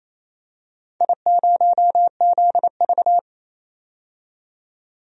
Morse code, heard as a keyed tone, about 28 wpm, 700 Hz. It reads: I0ZV
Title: DXCC-100 Call Signs - 4 Characters - 28wpm